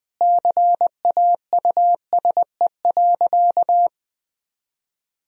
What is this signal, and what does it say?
Morse code, heard as a keyed tone, about 21 wpm, 700 Hz. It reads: CAUSE.
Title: Sentences from Top 1000 Words - Review Part 1 - 20wpm